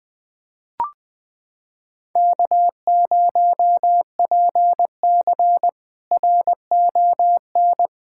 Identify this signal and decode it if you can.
Morse code, heard as a keyed tone, about 20 wpm, 700 Hz. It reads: K0PC RON
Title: K1USN SST QSO Practice - 20wpm